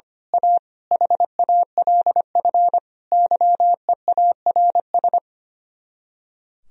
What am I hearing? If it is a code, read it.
Morse code, heard as a keyed tone, about 24 wpm, 700 Hz. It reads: A HALF YEARS
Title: Speed-Racing - Sets of 5 Words 25wpm